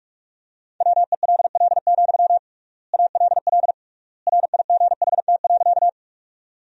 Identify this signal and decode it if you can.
Morse code, heard as a keyed tone, about 45 wpm, 700 Hz. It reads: WELL, ALL RIGHT.
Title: 3x Word-Spacing - Sentences from Top 200 Words - 45wpm